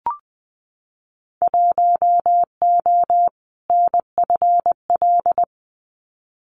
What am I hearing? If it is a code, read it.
Morse code, heard as a keyed tone, about 20 wpm, 700 Hz. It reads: 1O NFL
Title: Winter Field Day Exchanges 20wpm